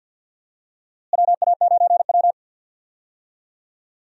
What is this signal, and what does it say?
Morse code, heard as a keyed tone, about 49 wpm, 700 Hz. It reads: WA9W
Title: USA Call Signs - 4 Characters - Repeated 3x - 50wpm